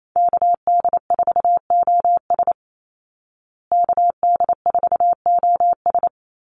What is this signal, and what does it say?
Morse code, heard as a keyed tone, about 28 wpm, 700 Hz. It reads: KD4OS KD4OS
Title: Call Signs Repeated 3x - 28wpm